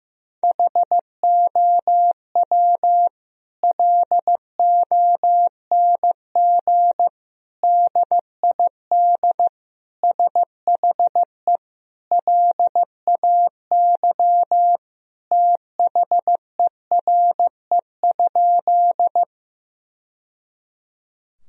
Morse code, heard as a keyed tone, about 15 wpm, 700 Hz. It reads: HOW LONG DID SHE LAY THERE?
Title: Sentences from Top 1000 Words - Review Part 2 - 15wpm